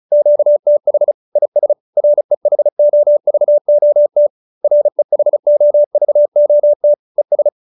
Morse code, heard as a keyed tone, about 34 wpm, 590 Hz. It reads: QTH IS REHOVOT REHOVOT ES